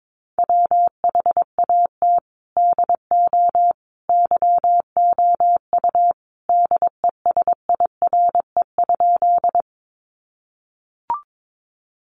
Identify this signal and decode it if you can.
Morse code, heard as a keyed tone, about 22 wpm, 700 Hz. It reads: WHAT DO YOU DESIRE?
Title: Sentences from Top 1100 Words 22wpm